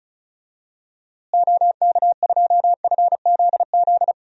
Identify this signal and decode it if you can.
Morse code, heard as a keyed tone, about 36 wpm, 700 Hz. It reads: OK2FZZ
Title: DXCC-100 Call Signs - 6 Characters - Repeated 3x - 35wpm